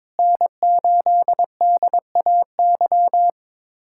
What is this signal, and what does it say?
Morse code, heard as a keyed tone, about 22 wpm, 700 Hz. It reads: N8DAY
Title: USA Callsign DE Callsign - 5 Characters - 22wpm